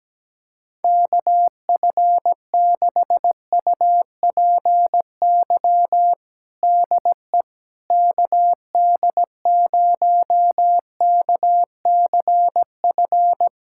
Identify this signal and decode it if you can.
Morse code, heard as a keyed tone, about 17 wpm, 700 Hz. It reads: KF6UPY DE KD0KCF
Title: USA Callsign DE Callsign - 6 Characters - 17wpm